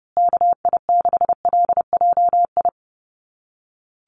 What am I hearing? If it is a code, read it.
Morse code, heard as a keyed tone, about 29 wpm, 700 Hz. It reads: KI6LJI
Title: USA Call Signs Repeated 2x 30wpm